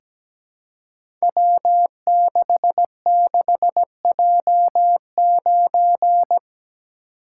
Morse code, heard as a keyed tone, about 17 wpm, 700 Hz. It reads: W66J9
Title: Code Groups - 5 Letter:Numbers - 17wpm